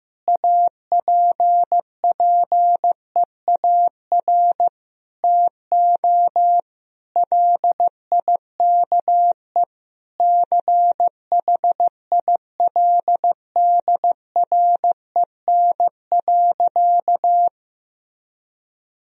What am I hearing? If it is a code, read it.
Morse code, heard as a keyed tone, about 15 wpm, 700 Hz. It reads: APPEAR TO LIKE CHILDREN.
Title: Sentences from Top 1000 Words - Review Part 2 - 15wpm